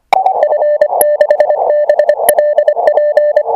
Morse code, heard as a keyed tone, about 25 wpm, 600 Hz, with radio noise. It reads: F6BLP